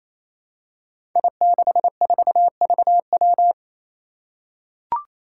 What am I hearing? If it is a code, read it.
Morse code, heard as a keyed tone, about 28 wpm, 700 Hz. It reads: I64VW
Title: Five Challenging Letter-Number Combinations 28wpm